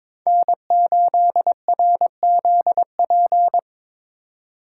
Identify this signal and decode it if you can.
Morse code, heard as a keyed tone, about 26 wpm, 700 Hz. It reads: N8RZP
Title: Speed Racing - USA 1x3 Call Signs 22wpm